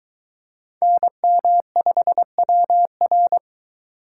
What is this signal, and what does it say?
Morse code, heard as a keyed tone, about 23 wpm, 700 Hz. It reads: NM5WR